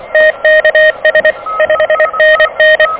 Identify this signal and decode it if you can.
Morse code, heard as a keyed tone, about 23 wpm, 590 Hz, with noise. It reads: TKS 5NN
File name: OK QRP 2023 / OK QRP 2023